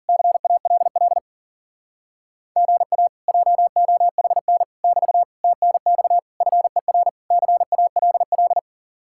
Morse code, heard as a keyed tone, about 52 wpm, 700 Hz. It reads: CALL GA JOHN <BT> TNX FER CALL